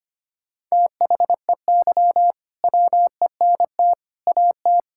Morse code, heard as a keyed tone, about 25 wpm, 700 Hz. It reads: THEY WENT AT
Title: Sentences from Top 1000 Words - Review Part 5 - 25wpm